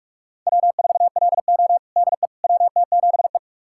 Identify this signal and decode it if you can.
Morse code, heard as a keyed tone, about 46 wpm, 700 Hz. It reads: W4LO DE WT7E
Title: USA Callsign DE Callsign - 4 Characters - 45wpm